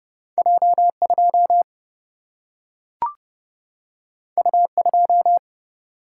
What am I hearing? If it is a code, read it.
Morse code, heard as a keyed tone, about 30 wpm, 700 Hz. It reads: J 2 U 2
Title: Single-Letter Single-Number 30wpm